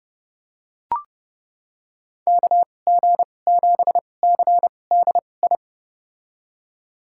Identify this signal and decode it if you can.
Morse code, heard as a keyed tone, about 31 wpm, spaced as for 22 wpm, 700 Hz. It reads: KG7CDI
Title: USA Call Signs - 6 Characters - 2x Character Spacing - 30wpm